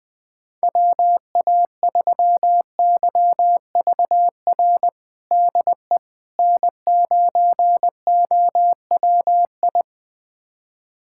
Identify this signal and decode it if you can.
Morse code, heard as a keyed tone, about 21 wpm, 700 Hz. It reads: WA3YVR DE N9OWI
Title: USA Callsign DE Callsign 20wpm